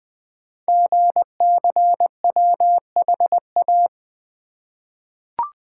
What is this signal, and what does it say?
Morse code, heard as a keyed tone, about 20 wpm, 700 Hz. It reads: GCWHA
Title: Code Groups - 5 Letter:Numbers - 20wpm